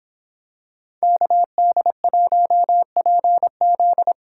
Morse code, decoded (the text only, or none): KD1PZ